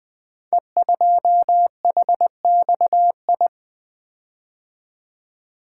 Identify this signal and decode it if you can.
Morse code, heard as a keyed tone, about 20 wpm, 700 Hz. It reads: E2HXI
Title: Code Groups - 5 Letter:Numbers - 20wpm